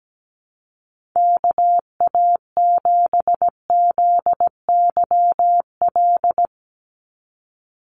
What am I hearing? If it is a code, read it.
Morse code, heard as a keyed tone, about 17 wpm, 700 Hz. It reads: KA7ZYL
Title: USA Call Signs 17wpm